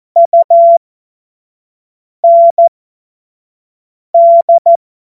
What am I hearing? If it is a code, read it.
Morse code, heard as a keyed tone, about 14 wpm, spaced as for 5 wpm, 680 Hz. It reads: UND
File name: Morsekurs / Witze / Witz 21. Juni 2022 25 F70